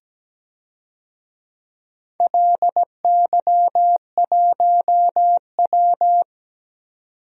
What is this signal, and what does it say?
Morse code, heard as a keyed tone, about 17 wpm, 700 Hz. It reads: LY1W
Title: DXCC-100 Call Signs - 4 Characters - 17wpm